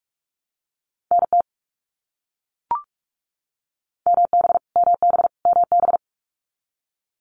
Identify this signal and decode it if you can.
Morse code, heard as a keyed tone, about 44 wpm, 700 Hz. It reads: NT MB MB MB
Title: Canadian Provinces and Territories - Repeated 3x - 45wpm